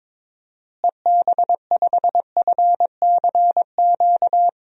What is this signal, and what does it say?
Morse code, heard as a keyed tone, about 22 wpm, 700 Hz. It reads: EB5FCQ
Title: DXCC-100 Call Signs - 6 Characters - Repeated 2x - 22wpm